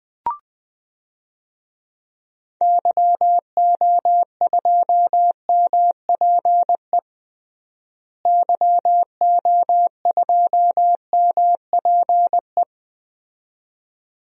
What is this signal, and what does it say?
Morse code, heard as a keyed tone, about 21 wpm, 700 Hz. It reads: YO2MPE YO2MPE
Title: DXCC-100 Call Signs - 6 Characters - Repeated 2x - 20wpm